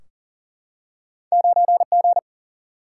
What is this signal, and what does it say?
Morse code, heard as a keyed tone, about 40 wpm, 700 Hz. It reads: 9G
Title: Single-Letter Single-Number 40wpm